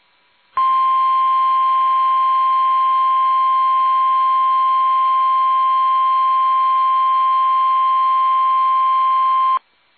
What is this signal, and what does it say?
Emergency Alert System (SAME) tones (the 1050 Hz attention tone). It is heard from 0.7 to 9.5 s.
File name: eas / KUSO / KUSO